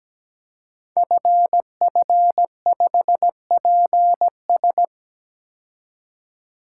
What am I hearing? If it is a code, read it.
Morse code, heard as a keyed tone, about 17 wpm, 700 Hz. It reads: FF5PS
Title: Code Groups - 5 Letter:Numbers - 17wpm